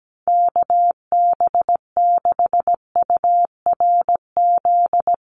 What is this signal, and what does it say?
Morse code, heard as a keyed tone, about 17 wpm, 700 Hz. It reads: KB6URZ
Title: USA Call Signs Repeated 3x 17wpm